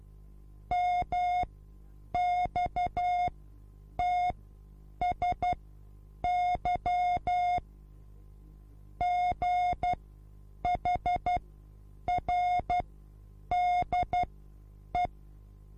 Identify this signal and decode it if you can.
Morse code, heard as a keyed tone, about 12 wpm, spaced as for 8 wpm, 720 Hz. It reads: MXTSY GHRDE